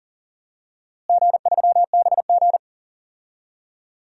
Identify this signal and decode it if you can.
Morse code, heard as a keyed tone, about 40 wpm, 700 Hz. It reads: G3BG
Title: DXCC-100 Call Signs - 4 Characters - Repeated 2x - 40wpm